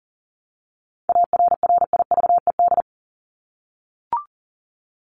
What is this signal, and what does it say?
Morse code, heard as a keyed tone, about 40 wpm, 700 Hz. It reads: ARRIVED
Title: ED Suffix - Top 100 Words with Base Words 40wpm